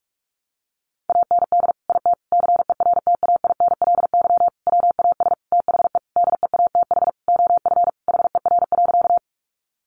Morse code, heard as a keyed tone, about 44 wpm, 700 Hz. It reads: AND IT CERTAINLY WAS THE DEATH OF HER.
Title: Sentences from Top 1100 Words 45wpm